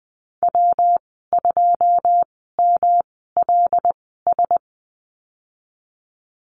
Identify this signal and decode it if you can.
Morse code, heard as a keyed tone, about 20 wpm, spaced as for 15 wpm, 700 Hz. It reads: W2MLS
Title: 122 Lesson 20g - Number 3 - Call Signs 20wpm